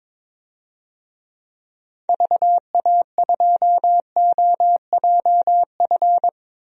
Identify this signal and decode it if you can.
Morse code, heard as a keyed tone, about 22 wpm, 700 Hz. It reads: VA2OJF
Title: DXCC-100 Call Signs - 6 Characters - Repeated 3x - 22wpm